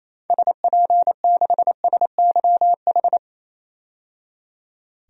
Morse code, heard as a keyed tone, about 28 wpm, 700 Hz. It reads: SP6SYH
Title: DXCC-100 Call Signs - 6 Characters - 28wpm